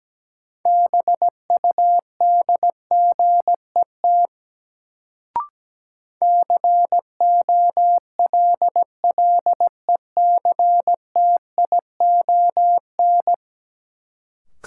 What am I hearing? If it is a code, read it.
Morse code, heard as a keyed tone, about 17 wpm, 700 Hz. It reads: BUDGET COLLECTION
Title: Sentences from Top 1200 Words - New Word Drill Alt - 17wpm